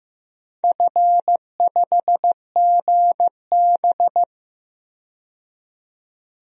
Morse code, heard as a keyed tone, about 15 wpm, 700 Hz. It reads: F5GB
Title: DXCC-100 Call Signs - 4 Characters - 15wpm